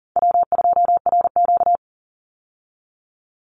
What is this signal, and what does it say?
Morse code, heard as a keyed tone, about 40 wpm, 700 Hz. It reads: W2RQ
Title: Call Signs Repeated 3x - 40wpm